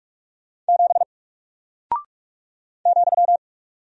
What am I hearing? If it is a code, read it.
Morse code, heard as a keyed tone, about 48 wpm, 700 Hz. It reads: 7 ,
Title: Single Character (Letter:Number:Punctuation:Prosign) Alt 45wpm